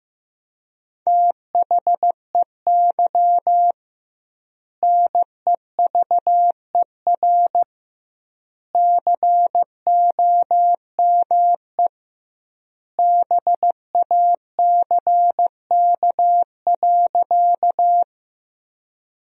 Morse code, heard as a keyed tone, about 15 wpm, 700 Hz. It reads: THEY NEVER COME BACK.
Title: 2x Word-Spacing - Sentences from Top 300 Words - 15wpm